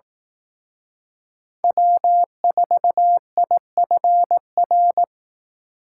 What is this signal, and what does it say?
Morse code, heard as a keyed tone, about 19 wpm, 700 Hz. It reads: W4IFR
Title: Speed Racing - USA 1x3 Call Signs 15wpm